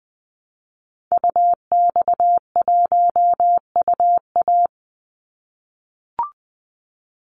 Morse code, heard as a keyed tone, about 20 wpm, 700 Hz. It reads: UX1UA
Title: Call Signs - 20wpm